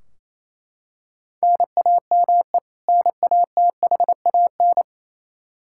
Morse code, heard as a keyed tone, about 28 wpm, 700 Hz. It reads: NAME NATHAN
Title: Practice QSO - Exchange 1 - Beginning - 28wpm